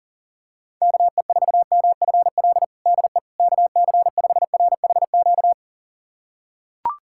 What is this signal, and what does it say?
Morse code, heard as a keyed tone, about 40 wpm, 700 Hz. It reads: KE4MFL DE KC5RHQ
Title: USA Callsign DE Callsign 40wpm